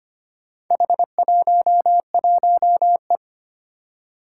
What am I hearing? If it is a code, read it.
Morse code, heard as a keyed tone, about 25 wpm, 700 Hz. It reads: H11E
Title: Four Challenging Letter-Number Combinations 25wpm